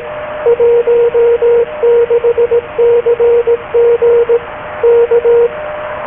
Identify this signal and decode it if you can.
Morse code, heard as a keyed tone, about 18 wpm, 470 Hz, with radio noise. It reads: 16CG K
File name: DAN-Audio / CQ-A1